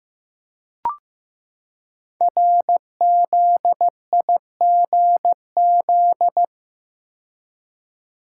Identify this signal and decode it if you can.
Morse code, heard as a keyed tone, about 15 wpm, 700 Hz. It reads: RZIGZ
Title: Code Groups - 5 Letters - 15wpm